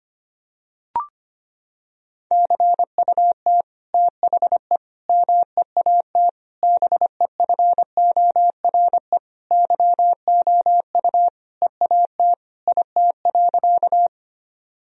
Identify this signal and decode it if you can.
Morse code, heard as a keyed tone, about 25 wpm, 700 Hz. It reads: CUT THE MEAT BEFORE YOU EAT IT.
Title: Sentences from Top 1000 Words - Review Part 3 - 25wpm